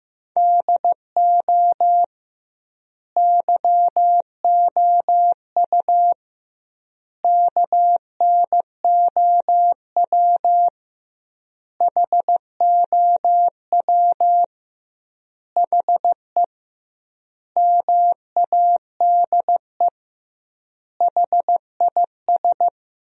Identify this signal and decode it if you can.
Morse code, heard as a keyed tone, about 16 wpm, 700 Hz. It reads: DO YOU KNOW HOW HE MADE HIS
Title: 2x Word-Spacing - Sentences from Top 300 Words - 15wpm